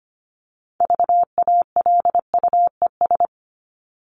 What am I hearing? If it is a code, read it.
Morse code, heard as a keyed tone, about 25 wpm, 700 Hz. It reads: VALUES
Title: ES:S Suffix - Top 100 Words with Base Words 25wpm